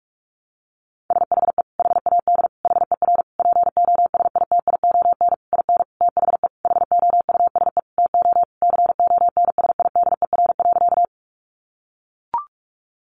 Morse code, heard as a keyed tone, about 44 wpm, 700 Hz. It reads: SHE HAD HER POSITION IN THE HOUSE TO CONSIDER.
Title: Sentences from Top 900 Words 45wpm